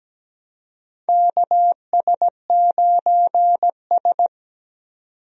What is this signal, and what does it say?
Morse code, heard as a keyed tone, about 17 wpm, 700 Hz. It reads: KS9S
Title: USA Call Signs - 4 Characters - 17wpm